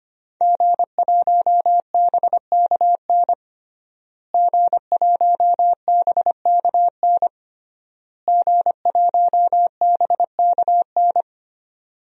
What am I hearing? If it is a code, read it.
Morse code, heard as a keyed tone, about 25 wpm, 700 Hz. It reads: G1BKN G1BKN G1BKN
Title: DXCC-100 Call Signs - 5 Characters - Repeated 3x - 25wpm